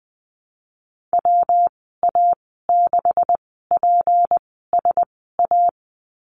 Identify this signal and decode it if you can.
Morse code, heard as a keyed tone, about 20 wpm, spaced as for 14 wpm, 700 Hz. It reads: WA6PSA
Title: 167 Lesson 27g - Number 7 - Call Signs 20wpm